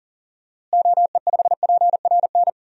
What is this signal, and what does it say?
Morse code, heard as a keyed tone, about 40 wpm, 700 Hz. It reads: OE5PRN
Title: DXCC-100 Call Signs - 6 Characters - Repeated 3x - 40wpm